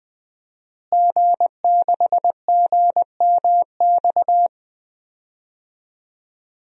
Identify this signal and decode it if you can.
Morse code, heard as a keyed tone, about 20 wpm, 700 Hz. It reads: G6GMX
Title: DXCC-100 Call Signs - 5 Characters - Repeated 3x - 20wpm